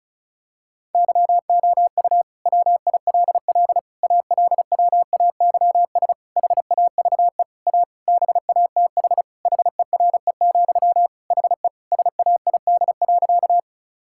Morse code, heard as a keyed tone, about 34 wpm, 700 Hz. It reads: YOU WILL ALWAYS HAVE A BATH HERE, HE SAID.
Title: Sentences from Top 1600 Words 35wpm